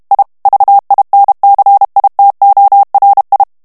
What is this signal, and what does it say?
Morse code, heard as a keyed tone, about 32 wpm, 800 Hz. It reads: I VINCITORI
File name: ITALIEN texte clair 2 vitesse 30 mots minute